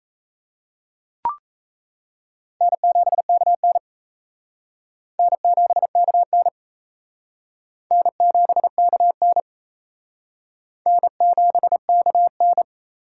Morse code, heard as a keyed tone, about 36 wpm, 700 Hz. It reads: N7KN N7KN N7KN N7KN